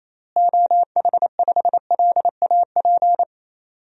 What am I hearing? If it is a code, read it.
Morse code, heard as a keyed tone, about 28 wpm, 700 Hz. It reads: OH5LAP
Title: DXCC-100 Call Signs - 6 Characters - Repeated 3x - 28wpm